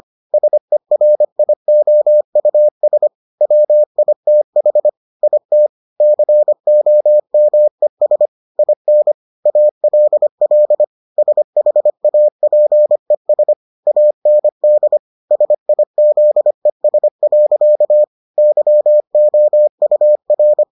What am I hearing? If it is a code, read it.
Morse code, heard as a keyed tone, about 25 wpm, 600 Hz. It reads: SERIOUS WITH IT COMES IN ALL SHAPES AND SIZES. YOUR